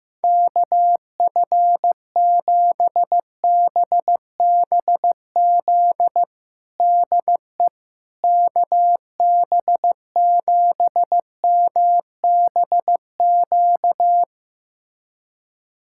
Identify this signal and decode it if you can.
Morse code, heard as a keyed tone, about 15 wpm, 700 Hz. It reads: KF7BBZ DE KB7MBQ
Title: USA Callsign DE Callsign - 6 Characters - 15wpm